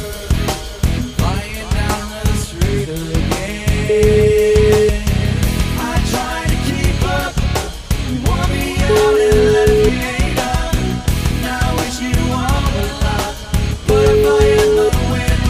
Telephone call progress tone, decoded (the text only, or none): ringback tone